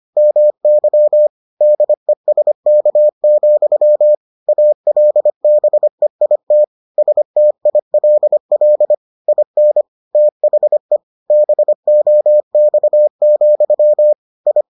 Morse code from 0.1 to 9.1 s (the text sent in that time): MY DESK, ALBEIT STILL